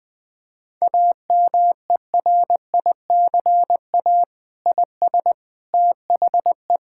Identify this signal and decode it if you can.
Morse code, heard as a keyed tone, about 20 wpm, 700 Hz. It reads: AMERICA IS THE
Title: Sentences with 8-Word Phrases - Spoken Beginning - 20wpm